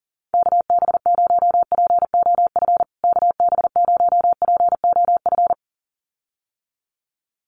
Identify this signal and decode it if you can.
Morse code, heard as a keyed tone, about 40 wpm, 700 Hz. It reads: KB0POF KB0POF
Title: USA Call Signs Repeated 2x 40wpm